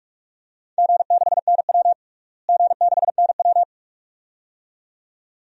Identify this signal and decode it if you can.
Morse code, heard as a keyed tone, about 47 wpm, 700 Hz. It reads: G6NW G6NW
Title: DXCC-100 Call Signs - 4 Characters - Repeated 2x - 45wpm